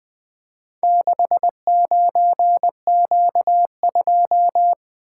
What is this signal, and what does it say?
Morse code, heard as a keyed tone, about 20 wpm, 700 Hz. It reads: 69Q2
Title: Four Letter-Number Combinations - 20wpm